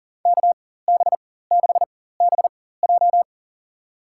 Morse code, heard as a keyed tone, about 40 wpm, spaced as for 19 wpm, 700 Hz. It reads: KB6BJ
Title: USA Call Signs - 4x Character Spacing - 40wpm